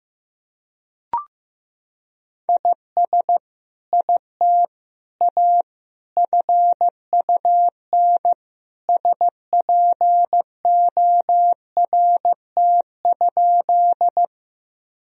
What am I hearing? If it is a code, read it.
Morse code, heard as a keyed tone, about 15 wpm, 700 Hz. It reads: IS IT A FUN SPORT?
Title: Sentences from Top 1000 Words - Review Part 4 - 15wpm